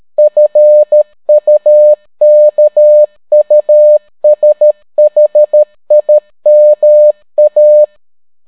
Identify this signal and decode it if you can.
Morse code, heard as a keyed tone, about 13 wpm, 600 Hz. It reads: FUKUSHIMA